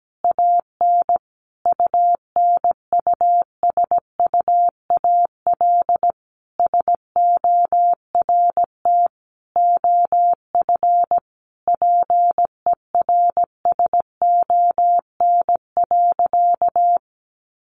Morse code, heard as a keyed tone, about 17 wpm, 700 Hz. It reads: AN UNUSUAL SORT OF PERSON.
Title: Sentences from Top 1400 Words 17wpm